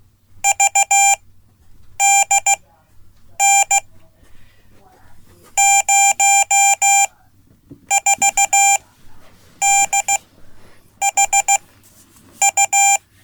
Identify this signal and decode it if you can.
Morse code, heard as a keyed tone, about 15 wpm, spaced as for 8 wpm, 790 Hz. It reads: VDN 04DHU
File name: Code / 5min-b